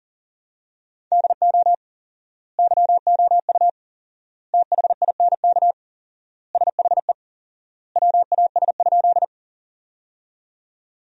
Morse code, heard as a keyed tone, about 39 wpm, 700 Hz. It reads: DO YOU THINK SHE WAS?
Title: 4x Word-Spacing - Sentences from Top 200 Words - 40wpm